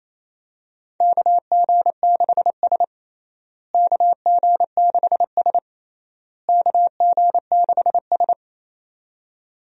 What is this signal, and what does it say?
Morse code, heard as a keyed tone, about 28 wpm, 700 Hz. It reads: KG6S KG6S KG6S
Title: USA Call Signs - 4 Characters - Repeated 3x - 28wpm